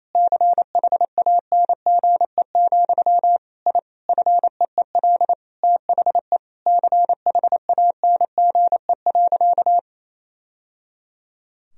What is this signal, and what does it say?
Morse code, heard as a keyed tone, about 27 wpm, 700 Hz. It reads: CHANGE, I FEEL THE CHANGE.
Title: Sentences from Top 1000 Words - Review Part 1 - 28wpm